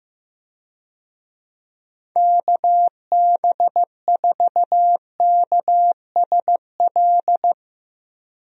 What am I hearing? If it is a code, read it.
Morse code, heard as a keyed tone, about 15 wpm, 700 Hz. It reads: KB4KSL
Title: USA Call Signs - 6 Characters - Repeated 3x - 15wpm